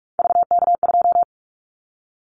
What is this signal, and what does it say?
Morse code, heard as a keyed tone, about 46 wpm, 700 Hz. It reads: VK2
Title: Callsign Prefixes for Top 50 DXCC - 45wpm